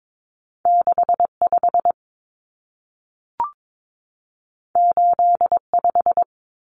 Morse code, heard as a keyed tone, about 22 wpm, 700 Hz. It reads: 65 85
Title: Numbers - 2-Digit 22wpm